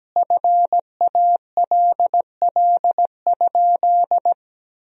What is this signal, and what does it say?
Morse code, heard as a keyed tone, about 17 wpm, 700 Hz. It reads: FALL?
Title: 2x Word-Spacing - Sentences from Top 500 Words - 17wpm